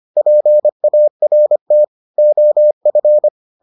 Morse code, heard as a keyed tone, about 25 wpm, 600 Hz. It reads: PART OF